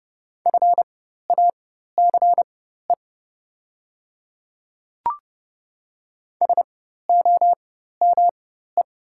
Morse code, heard as a keyed tone, about 31 wpm, spaced as for 14 wpm, 690 Hz. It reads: FACE SOME
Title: 4x Character Spacing - Top 500 Words - 30wpm